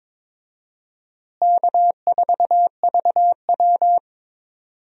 Morse code, heard as a keyed tone, about 22 wpm, 700 Hz. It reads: K4VW
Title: USA Call Signs - 4 Characters - Repeated 2x - 22wpm